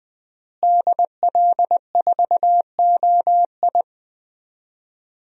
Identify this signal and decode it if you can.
Morse code, heard as a keyed tone, about 20 wpm, 700 Hz. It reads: DL4OI
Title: USA Call Signs - 5 Characters - Repeated 2x - 20wpm